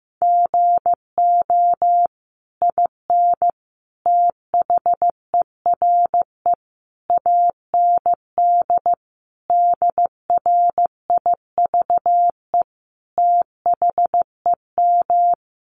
Morse code, heard as a keyed tone, about 15 wpm, 700 Hz. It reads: GO IN THERE AND DRIVE THEM
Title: Sentences from Top 500 Words 15wpm